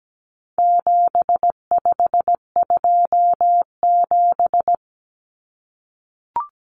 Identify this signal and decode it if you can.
Morse code, heard as a keyed tone, about 17 wpm, 700 Hz. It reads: 7527
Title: Numbers - 4-Digit 17wpm